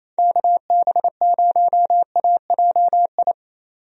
Morse code, heard as a keyed tone, about 27 wpm, 700 Hz. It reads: KB0AJI
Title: USA Call Signs - 6 Characters - 28wpm